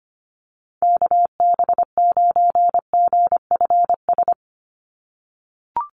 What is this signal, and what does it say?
Morse code, heard as a keyed tone, about 25 wpm, 690 Hz. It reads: KB9GFS
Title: USA Call Signs 25wpm